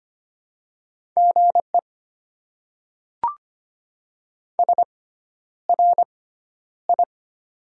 Morse code, heard as a keyed tone, about 25 wpm, spaced as for 11 wpm, 700 Hz. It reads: GE SRI
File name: Top 90 QSO Elements - 6x Character Spacing - 25wpm